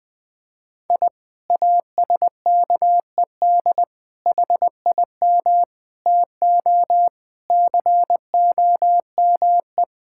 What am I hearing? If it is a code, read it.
Morse code, heard as a keyed tone, about 20 wpm, 700 Hz. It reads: I ASKED HIM TO COME
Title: Sentences with 2-Word Phrases - 20wpm